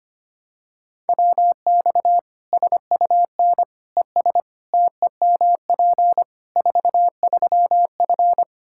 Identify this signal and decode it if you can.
Morse code, heard as a keyed tone, about 25 wpm, 700 Hz. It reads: WX SUN ES TEMP 43F
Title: Practice QSO - Exchange 2 - Info - 25wpm